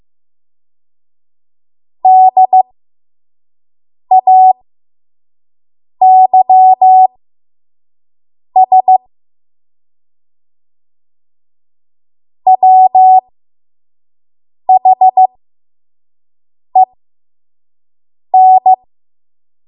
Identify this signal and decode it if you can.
Morse code, heard as a keyed tone, about 15 wpm, spaced as for 5 wpm, 750 Hz. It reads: DAYS WHEN
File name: Morse / 05WPM